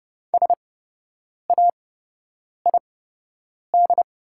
Morse code, heard as a keyed tone, about 30 wpm, spaced as for 8 wpm, 700 Hz. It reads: SAID
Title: 8x Character Spacing - Top 100 Words - 30wpm